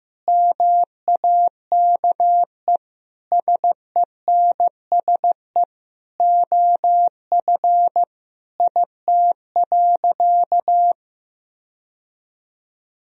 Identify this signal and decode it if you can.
Morse code, heard as a keyed tone, about 15 wpm, 700 Hz. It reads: MAKE SENSE OF IT.
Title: Sentences from Top 1000 Words - Review Part 4 - 15wpm